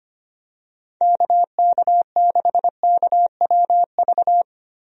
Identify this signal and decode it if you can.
Morse code, heard as a keyed tone, about 25 wpm, 700 Hz. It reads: KK6KWV
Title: USA Call Signs - 6 Characters - Repeated 3x - 25wpm